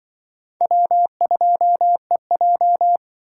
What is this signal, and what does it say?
Morse code, heard as a keyed tone, about 23 wpm, 700 Hz. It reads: W2EJ
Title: Speed Racing - USA 1x2 Call Signs 20wpm